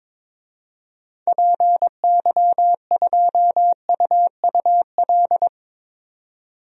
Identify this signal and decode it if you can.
Morse code, heard as a keyed tone, about 22 wpm, 700 Hz. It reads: PY2UUL
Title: DXCC-100 Call Signs - 6 Characters - 22wpm